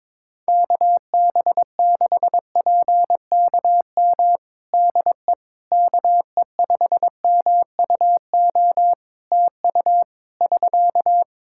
Morse code, heard as a keyed tone, about 22 wpm, 700 Hz. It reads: KB6PKM DE KE5MUO TU <SK>
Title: Practice QSO - Exchange 4 - Ending - 22wpm